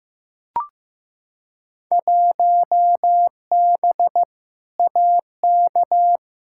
Morse code, heard as a keyed tone, about 15 wpm, 700 Hz. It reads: 1B AK
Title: Field Day Exchanges Encore - 15wpm